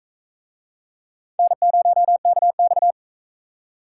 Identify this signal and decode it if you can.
Morse code, heard as a keyed tone, about 42 wpm, 700 Hz. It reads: N0KX